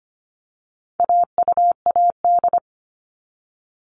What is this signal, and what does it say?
Morse code, heard as a keyed tone, about 25 wpm, 700 Hz. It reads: AUAD
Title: All Cut Numbers - 4-Digit 25wpm